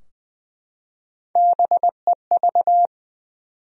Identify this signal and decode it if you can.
Morse code, heard as a keyed tone, about 20 wpm, 700 Hz. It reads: BEV
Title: USA Common Names - Max 3 Letters - 20wpm